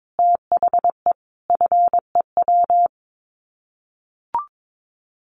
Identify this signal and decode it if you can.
Morse code, heard as a keyed tone, about 22 wpm, 700 Hz. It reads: THE FEW
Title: Sets of 5 Words Encore 22wpm